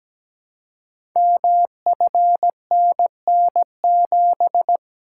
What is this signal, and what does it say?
Morse code, heard as a keyed tone, about 17 wpm, 700 Hz. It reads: MFNN7
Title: Code Groups - 5 Letter:Numbers - 17wpm